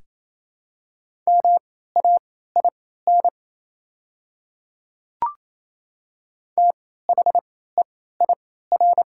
Morse code, heard as a keyed tone, about 28 wpm, spaced as for 16 wpm, 700 Hz. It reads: MAIN THEIR
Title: 3x Character Spacing - Top 300 Words - 28wpm